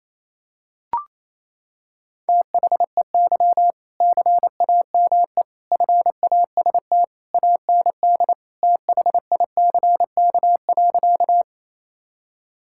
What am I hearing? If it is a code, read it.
Morse code, heard as a keyed tone, about 28 wpm, 700 Hz. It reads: THEY CAME FAST AND THICK.
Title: Sentences from Top 1600 Words 28wpm